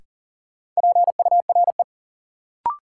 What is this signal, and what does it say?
Morse code, heard as a keyed tone, about 40 wpm, 700 Hz. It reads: PURE
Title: Sentences from Top 1500 Words - New Word Drill - 40wpm